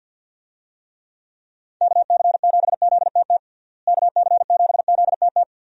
Morse code, heard as a keyed tone, about 50 wpm, 700 Hz. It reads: KC7ZTT KC7ZTT
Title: USA Call Signs - 6 Characters - Repeated 3x - 50wpm